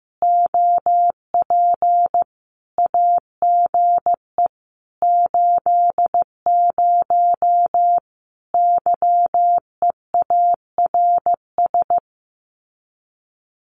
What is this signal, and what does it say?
Morse code, heard as a keyed tone, about 15 wpm, 700 Hz. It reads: OP AGE 80 YEARS
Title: Common QSO Phrases 15wpm